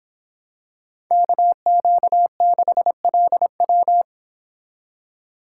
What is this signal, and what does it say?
Morse code, heard as a keyed tone, about 26 wpm, 700 Hz. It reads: KQ6LW
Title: Speed Racing - USA Call Signs 22wpm